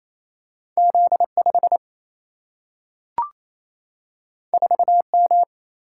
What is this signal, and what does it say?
Morse code, heard as a keyed tone, about 28 wpm, 700 Hz. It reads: Z 5 4 M
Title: Single-Letter Single-Number 28wpm